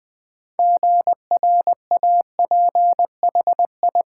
Morse code, heard as a keyed tone, about 20 wpm, 700 Hz. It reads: GRAPHI
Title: Build A Word - 2 Letters At A Time - 8-Letter Words - 20wpm